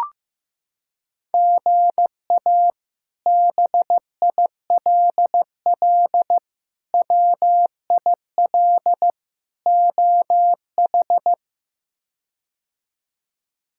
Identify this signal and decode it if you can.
Morse code, heard as a keyed tone, about 16 wpm, 700 Hz. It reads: GA BILL WIL OH
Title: K1USN SST QSO Practice - 15wpm